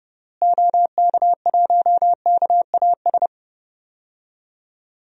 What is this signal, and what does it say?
Morse code, heard as a keyed tone, about 30 wpm, 700 Hz. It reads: OK1KAS
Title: DXCC-100 Call Signs - 6 Characters - Repeated 3x - 30wpm